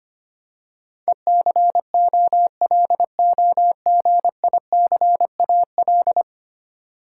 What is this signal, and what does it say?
Morse code, heard as a keyed tone, about 25 wpm, 700 Hz. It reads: ECOLOGICAL
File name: 10 Letter Words - First 3 Letters Given - 25wpm